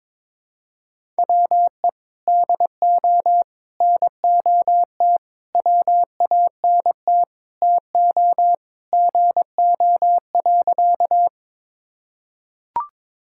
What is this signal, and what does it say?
Morse code, heard as a keyed tone, about 21 wpm, 700 Hz. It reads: WE DO NOT WANT TO GO.
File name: practice / 22wpm / Sentences with 4-Word Phrases - 22wpm